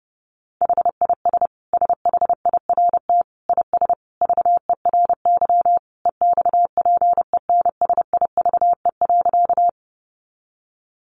Morse code, heard as a keyed tone, about 30 wpm, 700 Hz. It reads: HIS SHIRT IS VERY EXPENSIVE.
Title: Additional Easy Phrases 30wpm